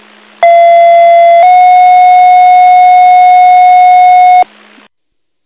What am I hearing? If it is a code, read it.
Two-tone page, heard as a one-second tone, then a three-second tone, about 680 then 730 Hz, with noise. It is heard from 0.4 to 4.4 s.